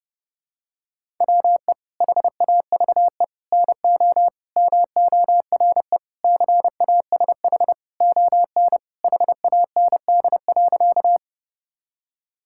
Morse code, heard as a keyed tone, about 30 wpm, 700 Hz. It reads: WE HAVE NO MORE CASH ON HAND.
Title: Sentences from Top 1000 Words - Review Part 5 - 30wpm